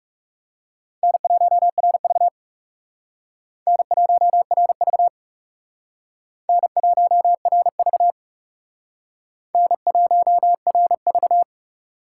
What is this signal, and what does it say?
Morse code, heard as a keyed tone, about 37 wpm, 700 Hz. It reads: N1RV N1RV N1RV N1RV
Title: Speed Racing - USA 1x2 Call Signs 30wpm